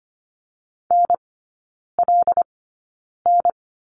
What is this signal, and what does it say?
Morse code, heard as a keyed tone, about 25 wpm, spaced as for 9 wpm, 690 Hz. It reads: NLN
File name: Lesson 15e - Number 5 - Three Characters 25wpm